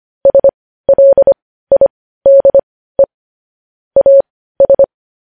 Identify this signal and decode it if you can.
Morse code, heard as a keyed tone, about 25 wpm, spaced as for 15 wpm, 550 Hz. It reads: SLIDE AS
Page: news headlines in morse code at 15 wpm